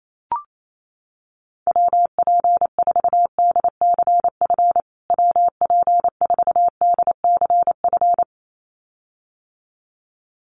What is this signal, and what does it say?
Morse code, heard as a keyed tone, about 28 wpm, 700 Hz. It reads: WP4DCF WP4DCF
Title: USA Call Signs Repeated 2x 28wpm